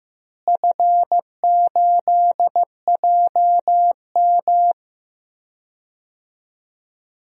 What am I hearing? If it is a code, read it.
Morse code, heard as a keyed tone, about 15 wpm, 700 Hz. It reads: F8JM
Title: DXCC-100 Call Signs - 4 Characters - Repeated 2x - 15wpm